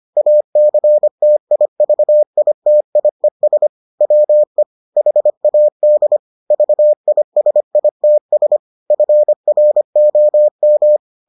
Morse code, heard as a keyed tone, about 25 wpm, 600 Hz. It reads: ACTIVITIES WE HAD VISITS FROM